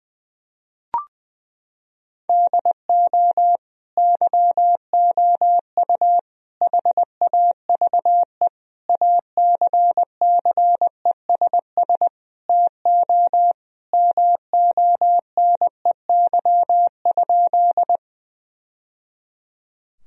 Morse code, heard as a keyed tone, about 20 wpm, 700 Hz. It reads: DO YOU HAVE ACCESS TO MONEY?
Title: Sentences from Top 1000 Words - Review Part 4 - 20wpm